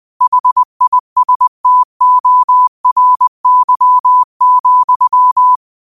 Morse code, heard as a keyed tone, about 20 wpm, 1000 Hz. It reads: HISTORY,
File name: qotdfiles / 20wpm